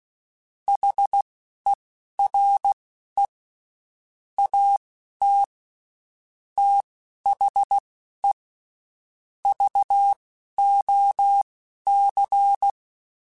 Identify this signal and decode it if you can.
Morse code, heard as a keyed tone, about 16 wpm, spaced as for 11 wpm, 800 Hz. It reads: HERE AT THE VOC
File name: (spaced) / QSOs Spaced-mp3s / WD0BGP 1 @15 sp